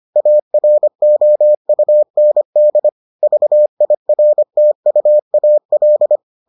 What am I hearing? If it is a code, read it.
Morse code, heard as a keyed tone, about 25 wpm, 600 Hz. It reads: AROUND VIRTUAL